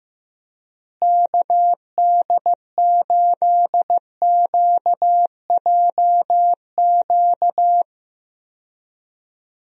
Morse code, heard as a keyed tone, about 16 wpm, 700 Hz. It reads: KD8QJQ
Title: USA Call Signs - 6 Characters - Repeated 2x - 15wpm